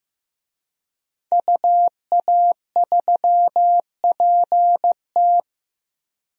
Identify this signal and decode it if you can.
Morse code, heard as a keyed tone, about 15 wpm, 700 Hz. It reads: UA3PT
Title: DXCC-100 Call Signs - 5 Characters - 15wpm